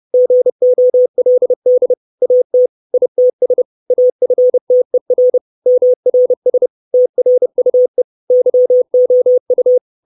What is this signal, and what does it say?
Morse code, heard as a keyed tone, about 30 wpm, 500 Hz. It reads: GOLD AT ITS AFTER MRS TRUE YOU
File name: 632 Words 30WPM